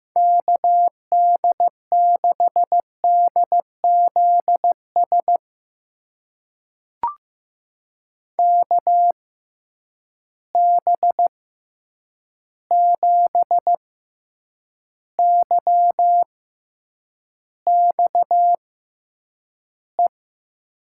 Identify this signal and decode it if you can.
Morse code, heard as a keyed tone, about 15 wpm, spaced as for 8 wpm, 700 Hz. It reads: KD6DZS KB7YXE
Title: USA Call Signs - 6 Characters - 6x Character Spacing - 15wpm